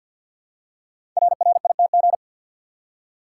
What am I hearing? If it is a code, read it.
Morse code, heard as a keyed tone, about 52 wpm, 700 Hz. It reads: RRITG
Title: Code Groups - 5 Letter:Numbers - 50wpm